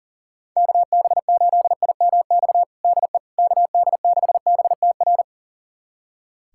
Morse code, heard as a keyed tone, about 37 wpm, 700 Hz. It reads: KB8IMX DE KD6BTR
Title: USA Callsign DE Callsign 40wpm